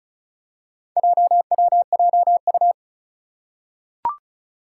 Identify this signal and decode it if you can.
Morse code, heard as a keyed tone, about 36 wpm, 700 Hz. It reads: JWJU
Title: Four Challenging Letter-Number Combinations 35wpm